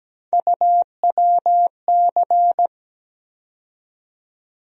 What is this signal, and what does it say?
Morse code, heard as a keyed tone, about 17 wpm, 700 Hz. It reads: UWC
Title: USA Call Sign Postfixes - 3 Letters 17wpm